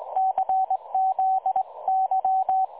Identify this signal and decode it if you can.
Morse code, heard as a keyed tone, about 20 wpm, 760 Hz, with radio noise. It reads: CZY
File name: dlazaawansowanych / nagraniazpasma / websdr recording 3565.0kHz